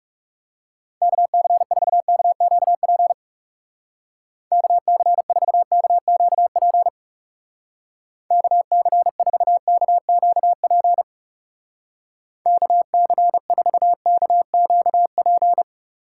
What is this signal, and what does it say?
Morse code, heard as a keyed tone, about 37 wpm, 700 Hz. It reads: KC4KQP KC4KQP KC4KQP KC4KQP
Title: Speed Racing - USA Call Signs 30wpm